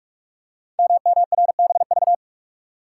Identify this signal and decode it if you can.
Morse code, heard as a keyed tone, about 44 wpm, 690 Hz. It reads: MMRBV
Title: Code Groups - 5 Letters - 45wpm